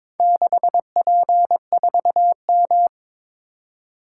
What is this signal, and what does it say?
Morse code, heard as a keyed tone, about 22 wpm, 700 Hz. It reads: 6P4M
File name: practice / 22wpm / Four Letter-Number Combinations - 22wpm